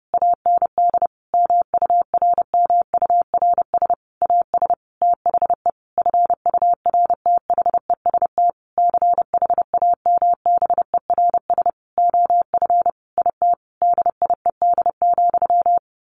Morse code, heard as a keyed tone, about 30 wpm, 700 Hz. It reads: AND MURMURS AS THE FURTHEST CHAMBERS OF IT DIED,
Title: A Sound Of Thunder 30wpm